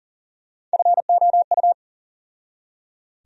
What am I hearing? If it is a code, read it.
Morse code, heard as a keyed tone, about 40 wpm, 700 Hz. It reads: FOU
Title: USA Call Sign Postfixes - 3 Letters 40wpm